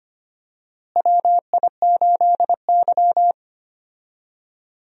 Morse code, heard as a keyed tone, about 25 wpm, 700 Hz. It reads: WI8Y
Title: USA Call Signs - 4 Characters - Repeated 3x - 25wpm